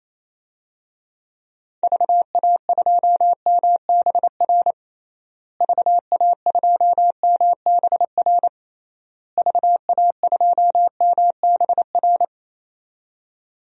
Morse code, heard as a keyed tone, about 27 wpm, 700 Hz. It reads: VA2MBR VA2MBR VA2MBR
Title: DXCC-100 Call Signs - 6 Characters - Repeated 3x - 28wpm